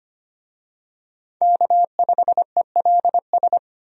Morse code, heard as a keyed tone, about 25 wpm, 700 Hz. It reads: K5ELS
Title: USA Call Signs - 5 Characters - 25wpm